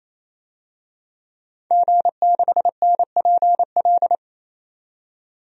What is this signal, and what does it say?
Morse code, heard as a keyed tone, about 28 wpm, 700 Hz. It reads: G6NPL
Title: DXCC-100 Call Signs - 5 Characters - 28wpm